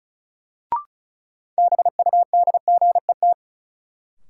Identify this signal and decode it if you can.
Morse code, heard as a keyed tone, about 34 wpm, 700 Hz. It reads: BUDGET
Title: Sentences from Top 1200 Words - New Word Drill - 35wpm